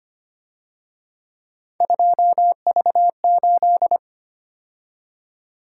Morse code, heard as a keyed tone, about 29 wpm, 700 Hz. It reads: 2V8
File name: Speed-Racing - Three Letter-Number Combinations - 25wpm